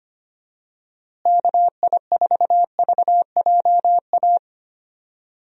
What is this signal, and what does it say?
Morse code, heard as a keyed tone, about 25 wpm, 700 Hz. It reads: KI4VJA
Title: USA Call Signs - 6 Characters - Repeated 2x - 25wpm